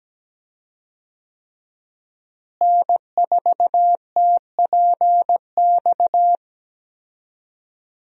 Morse code, heard as a keyed tone, about 17 wpm, 700 Hz. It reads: N4TPX
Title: USA Call Signs - 5 Characters - Repeated 3x - 17wpm